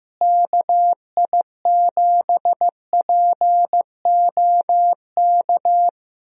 Morse code, heard as a keyed tone, about 15 wpm, 700 Hz. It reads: KI7POK
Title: USA Call Signs - 6 Characters - Repeated 3x - 15wpm